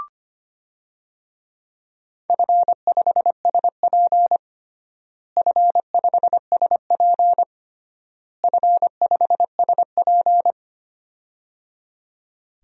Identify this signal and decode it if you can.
Morse code, heard as a keyed tone, about 25 wpm, 700 Hz. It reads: F5SP F5SP F5SP
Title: DXCC-100 Call Signs - 4 Characters - Repeated 3x - 25wpm